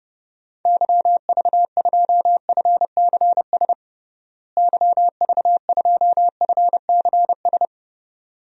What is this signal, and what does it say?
Morse code, heard as a keyed tone, about 30 wpm, 700 Hz. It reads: YV2FCS YV2FCS
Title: DXCC-100 Call Signs - 6 Characters - Repeated 2x - 30wpm